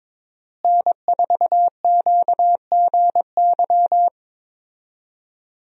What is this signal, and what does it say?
Morse code, heard as a keyed tone, about 22 wpm, 700 Hz. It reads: N4QGY
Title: USA Call Signs - 5 Characters - Repeated 2x - 22wpm